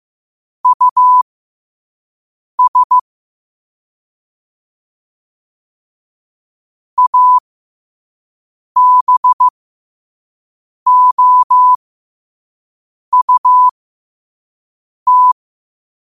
Morse code, heard as a keyed tone, about 15 wpm, spaced as for 5 wpm, 1000 Hz. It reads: US ABOUT
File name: 5wpm